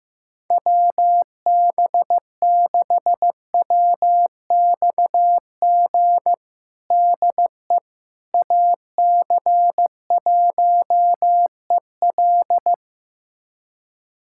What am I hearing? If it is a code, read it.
Morse code, heard as a keyed tone, about 15 wpm, 700 Hz. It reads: WB6WXG DE AC1EL
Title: USA Callsign DE Callsign 15wpm